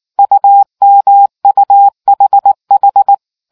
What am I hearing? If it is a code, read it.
Morse code, heard as a keyed tone, about 19 wpm, 800 Hz. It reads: UMUHH